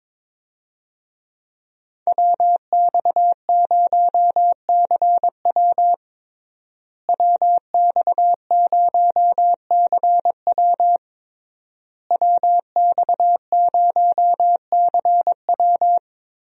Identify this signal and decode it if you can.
Morse code, heard as a keyed tone, about 22 wpm, 700 Hz. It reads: WX0CW WX0CW WX0CW
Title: USA Call Signs - 5 Characters - Repeated 3x - 22wpm